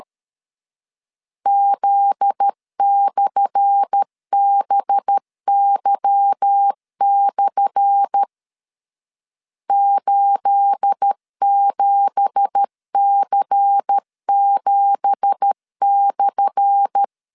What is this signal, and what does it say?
Morse code, heard as a keyed tone, about 13 wpm, 800 Hz. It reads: Z/BY/ 87C7/
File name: LEZIONI / 10 SIGNS / 10A SLASH 12wpm 2314